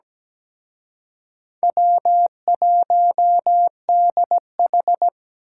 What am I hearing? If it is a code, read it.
Morse code, heard as a keyed tone, about 20 wpm, 700 Hz. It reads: W1DH
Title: Speed Racing - USA 1x2 Call Signs 17wpm